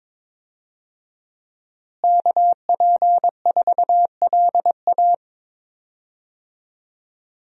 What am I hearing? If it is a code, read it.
Morse code, heard as a keyed tone, about 22 wpm, 700 Hz. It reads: KP4LA
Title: USA Call Signs - 5 Characters - Repeated 2x - 22wpm